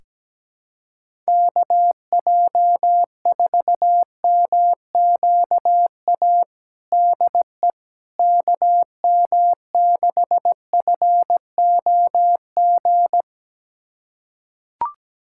Morse code, heard as a keyed tone, about 17 wpm, 700 Hz. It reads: KJ4MQA DE KM6FOG
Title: USA Callsign DE Callsign - 6 Characters - 17wpm